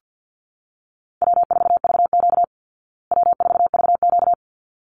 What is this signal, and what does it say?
Morse code, heard as a keyed tone, about 48 wpm, 700 Hz. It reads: W4VQ W4VQ
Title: Easy Call Signs Repeated 2x - 50wpm